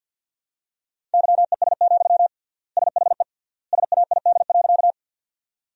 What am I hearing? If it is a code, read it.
Morse code, heard as a keyed tone, about 50 wpm, 700 Hz. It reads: YES, SHE SAID.
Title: 3x Word-Spacing - Sentences from Top 500 Words - 50wpm